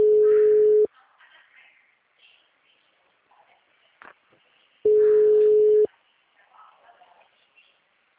Ringback tone (Europe, 425 Hz).